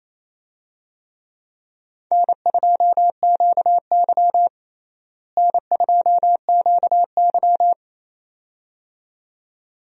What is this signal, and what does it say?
Morse code, heard as a keyed tone, about 28 wpm, 700 Hz. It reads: N2QY N2QY
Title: USA Call Signs - 4 Characters - Repeated 2x - 28wpm